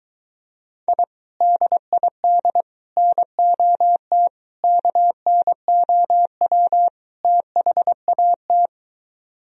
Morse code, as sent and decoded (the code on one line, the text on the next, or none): .. / -.. .. -.. / -. --- - / -.- -. --- .-- / - .... .- -
I DID NOT KNOW THAT